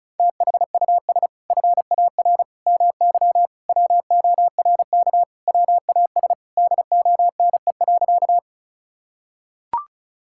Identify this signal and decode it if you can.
Morse code, heard as a keyed tone, about 34 wpm, 700 Hz. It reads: THUS FAR MY WORK WAS DONE.
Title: Sentences from Top 1000 Words - Review Part 5 - 35wpm